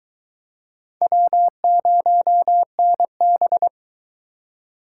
Morse code, heard as a keyed tone, about 22 wpm, 700 Hz. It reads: W0NB